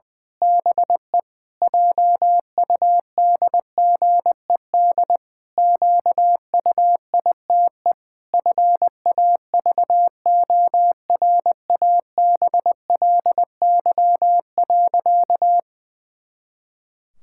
Morse code, heard as a keyed tone, about 20 wpm, 700 Hz. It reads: BE JUDGED QUITE FAVORABLY.
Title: Pangram Challenge v2 - 20wpm